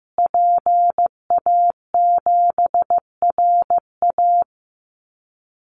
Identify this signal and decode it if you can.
Morse code, heard as a keyed tone, about 15 wpm, 700 Hz. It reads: PA7RA
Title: Call Signs Repeated 3x - 15wpm